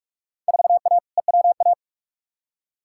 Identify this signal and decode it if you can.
Morse code, heard as a keyed tone, about 47 wpm, 700 Hz. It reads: 4A EWA
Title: Field Day Exchanges - 45wpm